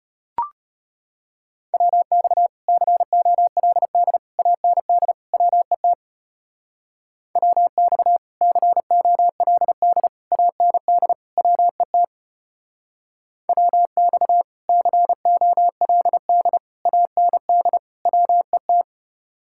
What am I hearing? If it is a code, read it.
Morse code, heard as a keyed tone, about 30 wpm, 700 Hz. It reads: WX COLD AND WET WX COLD AND WET WX COLD AND WET
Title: Speed-Racing Common QSO Phrases - 25wpm